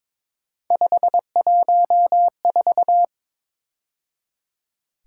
Morse code, heard as a keyed tone, about 26 wpm, 700 Hz. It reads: 514
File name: Speed-Racing - Three Letter-Number Combinations - 22wpm